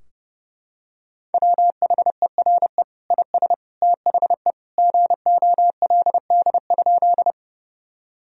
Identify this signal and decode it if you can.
Morse code, heard as a keyed tone, about 30 wpm, 700 Hz. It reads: WHERE IS THE GOLD?
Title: Sentences from Top 1000 Words - Review Part 3 - 30wpm